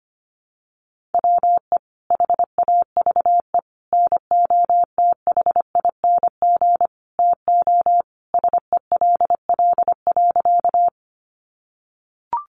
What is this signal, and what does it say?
Morse code, heard as a keyed tone, about 25 wpm, 700 Hz. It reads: WE HAVE NOTHING TO SELL.
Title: Sentences from Top 1100 Words 25wpm